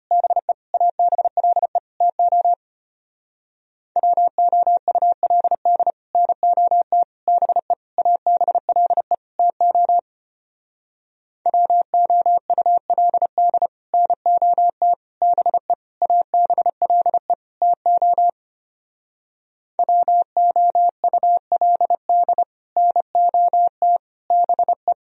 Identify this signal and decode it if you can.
Morse code, heard as a keyed tone, about 31 wpm, 700 Hz. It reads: BE ABLE TO WOULD NOT BE ABLE TO WOULD NOT BE ABLE TO WOULD NOT BE